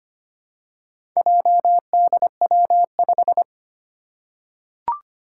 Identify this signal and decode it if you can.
Morse code, heard as a keyed tone, about 25 wpm, 700 Hz. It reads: JDW5
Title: Four Challenging Letter-Number Combinations 25wpm